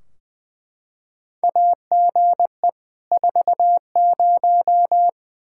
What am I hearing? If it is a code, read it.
Morse code, heard as a keyed tone, about 20 wpm, 700 Hz. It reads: AGE 40
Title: K1USN SST QSO Practice 20wpm